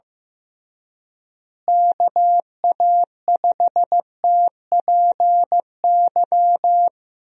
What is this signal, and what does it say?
Morse code, heard as a keyed tone, about 15 wpm, 700 Hz. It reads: KA5TPY
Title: USA Call Signs - 6 Characters - Repeated 3x - 15wpm